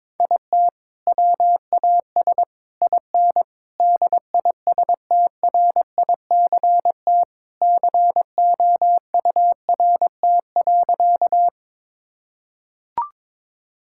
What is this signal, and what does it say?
Morse code, heard as a keyed tone, about 22 wpm, 700 Hz. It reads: IT WAS IN DISTRICT COURT.
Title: Sentences from Top 1600 Words 22wpm